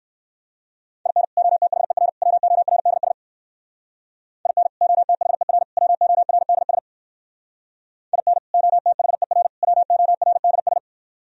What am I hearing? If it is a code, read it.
Morse code, heard as a keyed tone, about 62 wpm, 700 Hz. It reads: IN OTHER WORDS IN OTHER WORDS IN OTHER WORDS